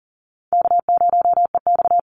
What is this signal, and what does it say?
Morse code, heard as a keyed tone, about 40 wpm, 700 Hz. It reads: K0EX
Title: Easy Call Signs - 40wpm